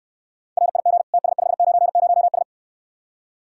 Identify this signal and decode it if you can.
Morse code, heard as a keyed tone, about 69 wpm, 700 Hz. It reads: RIG TS590S